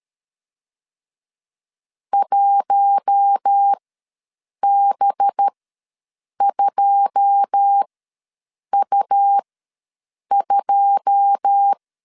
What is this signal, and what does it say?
Morse code, heard as a keyed tone, about 13 wpm, spaced as for 7 wpm, 800 Hz. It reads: 1B2U2
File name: LEZIONI / 9 2 ws 12wpm 3406